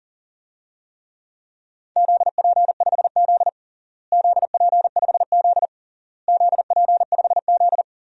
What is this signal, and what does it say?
Morse code, heard as a keyed tone, about 40 wpm, 700 Hz. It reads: ZP5Z ZP5Z ZP5Z
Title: DXCC-100 Call Signs - 4 Characters - Repeated 3x - 40wpm